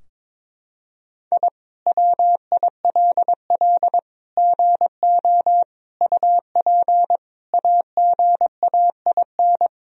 Morse code, heard as a keyed tone, about 21 wpm, 700 Hz. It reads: I WILL GO UP AGAIN
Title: Sentences from Top 1000 Words - Review Part 3 - 22wpm